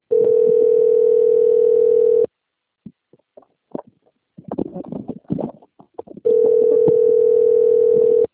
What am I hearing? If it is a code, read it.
Ringback tone (North America), 440 + 480 Hz.